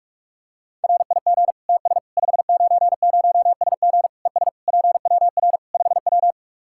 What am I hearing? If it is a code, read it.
Morse code, heard as a keyed tone, about 46 wpm, 700 Hz. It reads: RIG TS 590SG ES PWR 5W
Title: Practice QSO - Exchange 2 - Info - 45wpm